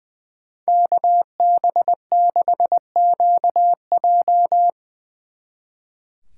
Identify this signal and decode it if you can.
Morse code, heard as a keyed tone, about 23 wpm, 700 Hz. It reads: KB6QJ
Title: Speed Racing - USA 2x2 Call Signs 20wpm